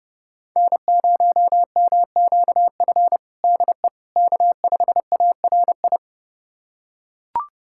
Morse code, heard as a keyed tone, about 30 wpm, 700 Hz. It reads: N0MQF DE K5ARI
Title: USA Callsign DE Callsign 30wpm